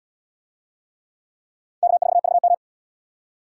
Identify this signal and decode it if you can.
Morse code, heard as a keyed tone, about 74 wpm, 700 Hz. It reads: K4VM